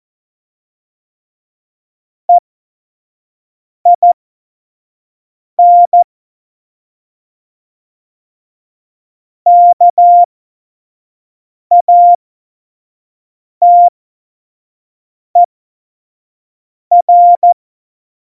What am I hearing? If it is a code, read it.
Morse code, heard as a keyed tone, about 14 wpm, spaced as for 5 wpm, 700 Hz. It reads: EIN KATER
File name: Morsekurs / Witze / Witz 23. Juli 2019, 25 F70